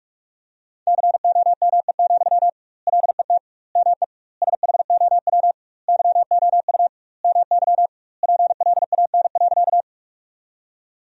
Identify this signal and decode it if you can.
Morse code, heard as a keyed tone, about 46 wpm, 700 Hz. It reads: COME, LET ME SHOW YOU MY PLAN.
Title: 2x Word-Spacing - Sentences from Top 500 Words - 45wpm